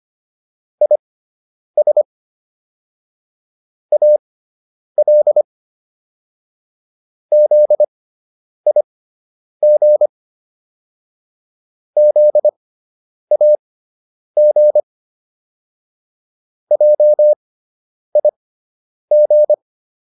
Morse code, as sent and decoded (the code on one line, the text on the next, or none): .. ... / .- .-.. / --.. .. --. / --.. .- --. / .--- .. --.
IS AL ZIG ZAG JIG